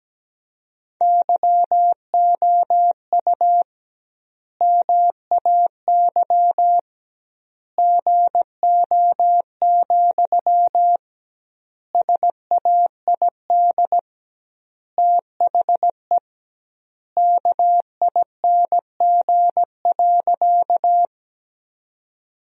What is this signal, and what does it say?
Morse code, heard as a keyed tone, about 17 wpm, 700 Hz. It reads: YOU MAY GO, SAID THE KING.
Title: 2x Word-Spacing - Sentences from Top 400 Words - 17wpm